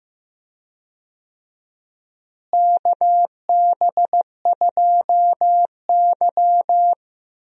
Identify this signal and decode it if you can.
Morse code, heard as a keyed tone, about 15 wpm, 700 Hz. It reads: KB2Y
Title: USA Call Signs - 4 Characters - Repeated 3x - 15wpm